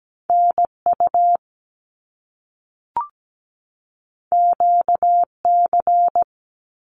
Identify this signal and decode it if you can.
Morse code, heard as a keyed tone, about 17 wpm, 700 Hz. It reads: NU QC
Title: Canadian Provinces and Territories 17wpm